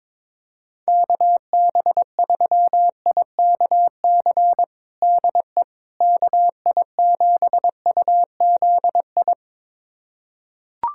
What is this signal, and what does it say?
Morse code, heard as a keyed tone, about 22 wpm, 700 Hz. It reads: KB3IKC DE KI7UZI
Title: USA Callsign DE Callsign - 6 Characters - 22wpm